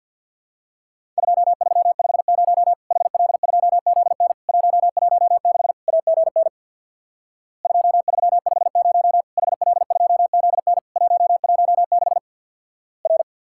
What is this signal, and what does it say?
Morse code, heard as a keyed tone, about 50 wpm, 700 Hz. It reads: 2350 HL2ZN 116 AGN 2350 HL2ZN 116 R
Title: Worked All Europe DX Contest - QTC Exchange with Repeats - 50wpm